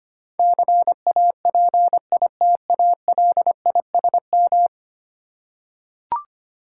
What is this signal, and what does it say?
Morse code, heard as a keyed tone, about 25 wpm, 700 Hz. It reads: CAPITALISM
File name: 10 Letter Words - First 3 Letters Given - 25wpm